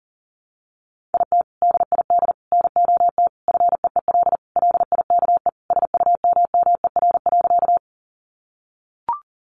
Morse code, heard as a keyed tone, about 40 wpm, 700 Hz. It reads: IT DID NOT FEEL LIKE SUMMER.
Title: Sentences from Top 1100 Words 40wpm